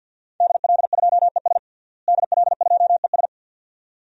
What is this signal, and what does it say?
Morse code, heard as a keyed tone, about 51 wpm, 700 Hz. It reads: DL2ES DL2ES
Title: DXCC-100 Call Signs - 5 Characters - Repeated 2x - 50wpm